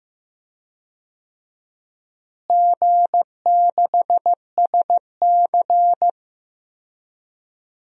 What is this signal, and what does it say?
Morse code, heard as a keyed tone, about 15 wpm, 700 Hz. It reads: G6SC
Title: DXCC-100 Call Signs - 4 Characters - Repeated 3x - 15wpm